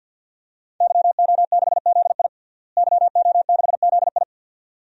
Morse code, heard as a keyed tone, about 52 wpm, 700 Hz. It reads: YO6ZI YO6ZI
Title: DXCC-100 Call Signs - 5 Characters - Repeated 2x - 50wpm